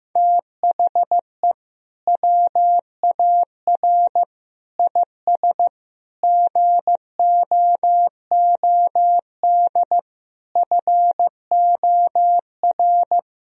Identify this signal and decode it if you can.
Morse code, heard as a keyed tone, about 15 wpm, 700 Hz. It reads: THE WAR IS GOOD FOR
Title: Sentences from Top 1000 Words - Review Part 4 - 15wpm